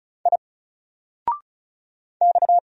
Morse code, heard as a keyed tone, about 33 wpm, 700 Hz. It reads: I X
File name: Single-Letter-Number-35wpm